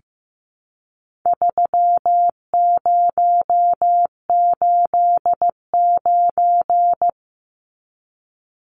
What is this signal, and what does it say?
Morse code, heard as a keyed tone, about 15 wpm, 700 Hz. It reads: 3089
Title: Numbers - 4-Digit 15wpm